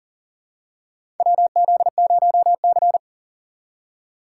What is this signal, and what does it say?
Morse code, heard as a keyed tone, about 40 wpm, 700 Hz. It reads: WZ0C
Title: USA Call Signs - 4 Characters - 3x Character Spacing - 40wpm